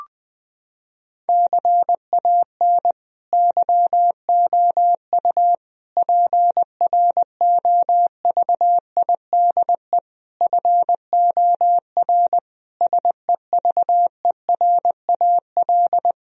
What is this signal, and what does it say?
Morse code, heard as a keyed tone, about 20 wpm, 700 Hz. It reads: CAN YOU PROVIDE FOR SEVERAL
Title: Sentences from Top 1000 Words - Review Part 2 - 20wpm